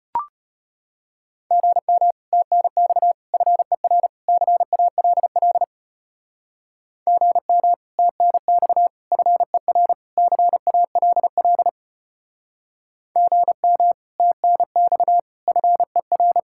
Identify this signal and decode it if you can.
Morse code, heard as a keyed tone, about 31 wpm, 700 Hz. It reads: GM TNX FER CALL GM TNX FER CALL GM TNX FER
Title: Speed-Racing Common QSO Phrases Encore - 25wpm